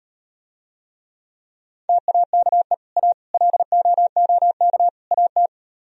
Morse code, heard as a keyed tone, about 38 wpm, 700 Hz. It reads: TAKE A LOOK AT